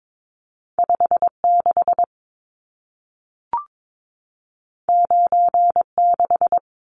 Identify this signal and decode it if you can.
Morse code, heard as a keyed tone, about 22 wpm, 700 Hz. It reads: 56 96
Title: Numbers - 2-Digit 22wpm